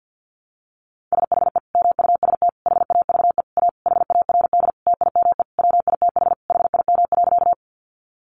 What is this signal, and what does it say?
Morse code, heard as a keyed tone, about 50 wpm, 700 Hz. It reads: SHE MUST HAVE A HARD TIME WITH HIM.
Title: Sentences from Top 300 Words 50wpm